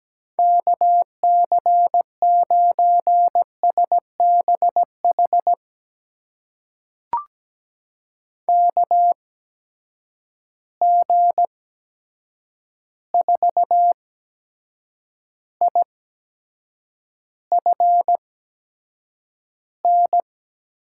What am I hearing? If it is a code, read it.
Morse code, heard as a keyed tone, about 17 wpm, spaced as for 7 wpm, 700 Hz. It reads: KC9SBH KG4IFN
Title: USA Call Signs - 8x Character Spacing - 17wpm